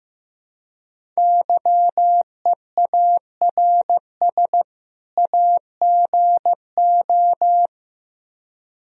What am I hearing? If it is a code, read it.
Morse code, heard as a keyed tone, about 16 wpm, 700 Hz. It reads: YEARS AGO
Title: Speed-Racing - Sets of 2 Words 15wpm